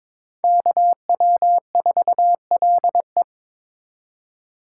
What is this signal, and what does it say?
Morse code, heard as a keyed tone, about 26 wpm, 700 Hz. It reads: KW4LE
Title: Speed Racing - USA 2x2 Call Signs 22wpm